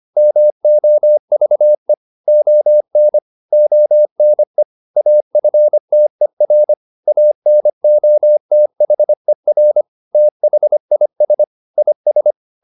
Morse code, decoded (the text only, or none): MOVE ON ONE AFTER ANOTHER THIS IS